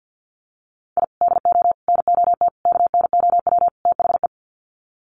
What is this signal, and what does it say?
Morse code, heard as a keyed tone, about 52 wpm, 700 Hz. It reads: I DO NOT KNOW THE
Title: Sets of 5 Words 50wpm